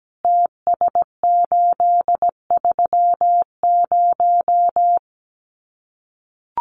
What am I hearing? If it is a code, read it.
Morse code, heard as a keyed tone, about 17 wpm, 700 Hz. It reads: TS830
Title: Common QSO Phrases 17wpm